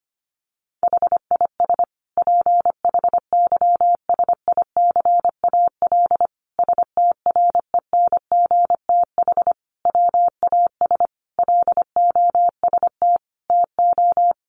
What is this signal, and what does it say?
Morse code, heard as a keyed tone, about 25 wpm, 700 Hz. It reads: HIS PHYSICAL STRENGTH WAS LOST TO
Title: Sentences from Top 1400 Words 25wpm